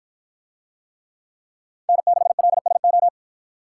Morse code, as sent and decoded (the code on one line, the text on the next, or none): -. -.... .-.. ... ---
N6LSO